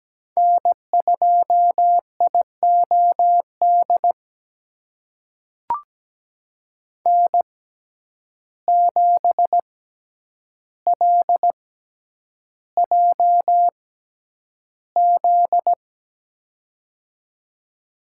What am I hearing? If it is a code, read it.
Morse code, heard as a keyed tone, about 17 wpm, spaced as for 9 wpm, 700 Hz. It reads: N2IOD N7LJZ
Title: USA Call Signs - 5 Characters - 6x Character Spacing - 17wpm